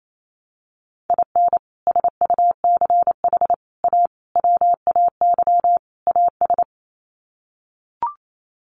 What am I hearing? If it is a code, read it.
Morse code, heard as a keyed tone, about 28 wpm, 700 Hz. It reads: IN SUCH A WAY AS
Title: Sets of 5 Words Encore 28wpm